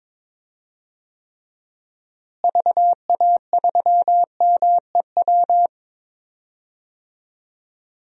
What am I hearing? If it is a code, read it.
Morse code, heard as a keyed tone, about 22 wpm, 700 Hz. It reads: VA3MEW
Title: DXCC-100 Call Signs - 6 Characters - Repeated 3x - 22wpm